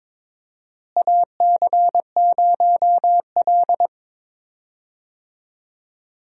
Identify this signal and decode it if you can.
Morse code, heard as a keyed tone, about 21 wpm, 700 Hz. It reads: AC0L
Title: USA Call Signs - 4 Characters - Repeated 3x - 22wpm